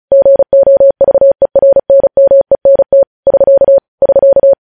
Morse code, heard as a keyed tone, about 36 wpm, 550 Hz. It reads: GOVERNMENT <SK> <SK>
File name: podcast / News headlines-1x-35x35